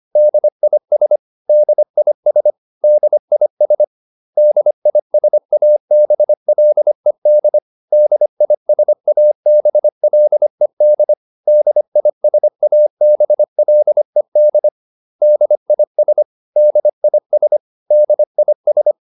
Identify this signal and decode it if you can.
Morse code, heard as a keyed tone, about 25 wpm, 600 Hz. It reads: DIS DIS DIS DISABLED DISABLED DISABLED DIS DIS DIS
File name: DIS8 25